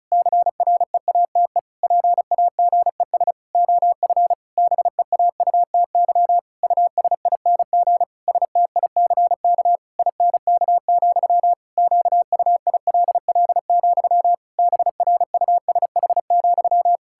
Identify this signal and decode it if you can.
Morse code, heard as a keyed tone, about 36 wpm, 700 Hz. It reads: CREATE PAGES OF BEAUTY USING STICK INK, QUILL, BRUSH,
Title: Pangram Challenge v2 - 35wpm